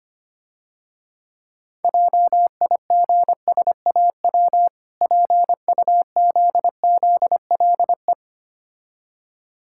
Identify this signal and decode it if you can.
Morse code, heard as a keyed tone, about 26 wpm, 700 Hz. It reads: JIGSAW PUZZLE
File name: practice / 25wpm / Difficult Words - Speed-Racing with Hints - 25wpm